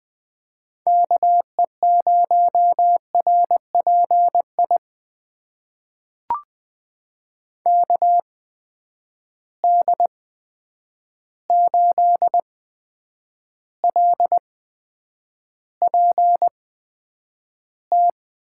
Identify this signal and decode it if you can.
Morse code, heard as a keyed tone, about 21 wpm, spaced as for 9 wpm, 700 Hz. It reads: KE0RPI KD8LPT
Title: USA Call Signs - 6 Characters - 8x Character Spacing - 20wpm